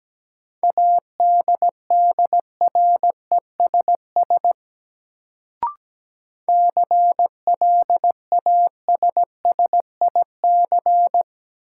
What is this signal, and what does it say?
Morse code, heard as a keyed tone, about 17 wpm, 700 Hz. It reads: ADDRESS CLASSIC
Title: Sentences from Top 1000 Words - New Word Drill - 17wpm